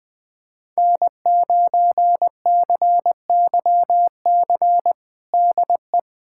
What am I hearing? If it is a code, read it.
Morse code, heard as a keyed tone, about 20 wpm, 700 Hz. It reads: N9CYC DE
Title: USA Callsign DE Callsign - 5 Characters - 20wpm